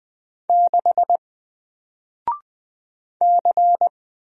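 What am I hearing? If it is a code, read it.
Morse code, heard as a keyed tone, about 20 wpm, 700 Hz. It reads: 6 C
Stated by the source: Single letter and number Morse drill at 20 wpm, from N4LCD.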